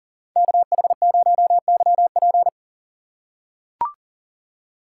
Morse code, heard as a keyed tone, about 40 wpm, 700 Hz. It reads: KH0YP
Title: USA Call Signs - 3x Character Spacing - 40wpm